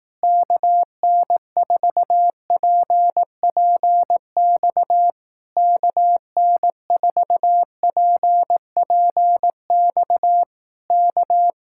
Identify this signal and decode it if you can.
Morse code, heard as a keyed tone, about 18 wpm, 700 Hz. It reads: KN4PPX KN4PPX K
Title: Speed-Racing Common QSO Phrases Encore - 15wpm